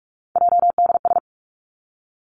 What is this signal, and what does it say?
Morse code, heard as a keyed tone, about 46 wpm, 700 Hz. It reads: JDS
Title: Three Challenging Letter-Number Combinations 45wpm